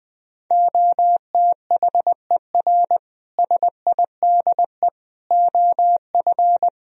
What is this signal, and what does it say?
Morse code, heard as a keyed tone, about 20 wpm, 700 Hz. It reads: OTHER SIDE OF
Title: Speed-Racing - Sets of 4 Words 20wpm